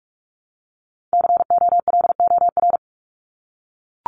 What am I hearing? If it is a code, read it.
Morse code, heard as a keyed tone, about 44 wpm, 700 Hz. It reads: COLOR
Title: Top 500 Words 45wpm